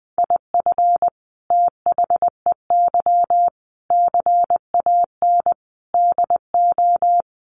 Morse code, heard as a keyed tone, about 20 wpm, 700 Hz. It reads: IF THEY CAN DO
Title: Sentences from Top 600 Words 20wpm